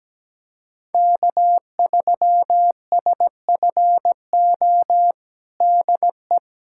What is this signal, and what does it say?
Morse code, heard as a keyed tone, about 17 wpm, 700 Hz. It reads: K3SFO DE
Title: USA Callsign DE Callsign - 5 Characters - 17wpm